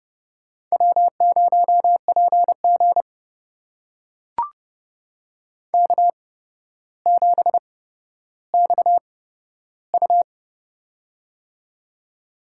Morse code, heard as a keyed tone, about 30 wpm, spaced as for 13 wpm, 700 Hz. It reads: W0PG K7XU
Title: USA Call Signs - 4 Characters - 8x Character Spacing - 30wpm